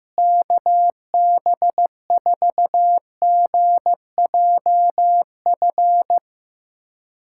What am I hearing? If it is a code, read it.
Morse code, heard as a keyed tone, about 15 wpm, 700 Hz. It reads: KB4GJF
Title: USA Callsign DE Callsign 15wpm